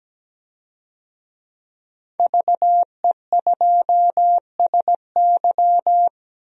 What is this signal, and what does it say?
Morse code, heard as a keyed tone, about 17 wpm, 700 Hz. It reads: VE2SY
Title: DXCC-100 Call Signs - 5 Characters - 17wpm